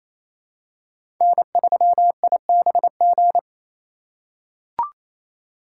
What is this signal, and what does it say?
Morse code, heard as a keyed tone, about 28 wpm, 700 Hz. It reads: N3IBG
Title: USA Call Signs - 5 Characters - 2x Character Spacing - 28wpm